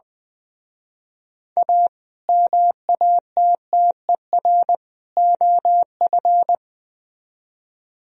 Morse code, heard as a keyed tone, about 23 wpm, 700 Hz. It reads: A MATTER OF
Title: Speed-Racing - Sets of 3 Words 20wpm